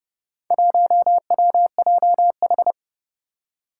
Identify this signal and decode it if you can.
Morse code, heard as a keyed tone, about 30 wpm, 700 Hz. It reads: 1WJH
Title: Four Challenging Letter-Number Combinations 30wpm